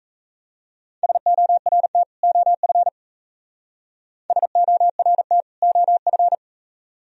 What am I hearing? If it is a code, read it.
Morse code, heard as a keyed tone, about 32 wpm, 700 Hz. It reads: SORT OF SORT OF
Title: Speed-Racing - Sets of 2 Words 28wpm